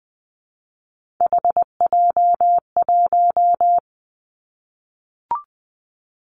Morse code, heard as a keyed tone, about 20 wpm, 700 Hz. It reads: HJ1
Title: Three Challenging Letter-Number Combinations 20wpm